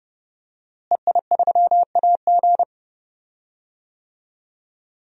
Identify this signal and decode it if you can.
Morse code, heard as a keyed tone, about 30 wpm, 700 Hz. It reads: EI3AG
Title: DXCC-100 Call Signs - 5 Characters - Repeated 3x - 30wpm